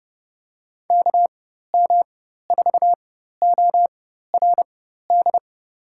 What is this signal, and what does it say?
Morse code, heard as a keyed tone, about 30 wpm, spaced as for 14 wpm, 700 Hz. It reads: KM4ORD
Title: USA Call Signs - 4x Character Spacing - 30wpm